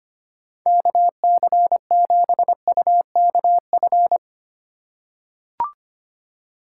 Morse code, heard as a keyed tone, about 25 wpm, 700 Hz. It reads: KC7UKF
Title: USA Call Signs - 6 Characters - 3x Character Spacing - 25wpm